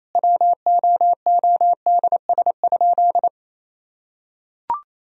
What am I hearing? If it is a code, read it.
Morse code, heard as a keyed tone, about 27 wpm, 700 Hz. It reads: WOODS?
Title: Sentences from Top 1000 Words - Review Part 2 - 28wpm